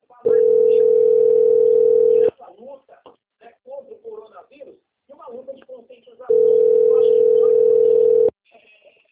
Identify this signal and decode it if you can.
Ringback tone (North America), 440 + 480 Hz, with speech.
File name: out- - - - - .20402